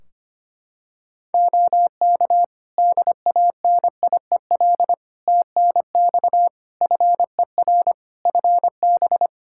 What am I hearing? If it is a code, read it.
Morse code, heard as a keyed tone, about 25 wpm, 700 Hz. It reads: OK DANIEL TNX FER FB
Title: K1USN SST QSO Practice 25wpm